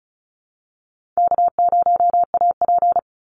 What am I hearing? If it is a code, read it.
Morse code, heard as a keyed tone, about 34 wpm, 700 Hz. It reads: K0AP
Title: Call Signs - 35wpm